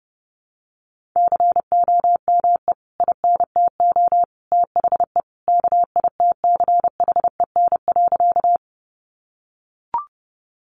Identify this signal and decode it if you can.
Morse code, heard as a keyed tone, about 30 wpm, 700 Hz. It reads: COME INTO THE KITCHEN.
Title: Sentences from Top 1400 Words 30wpm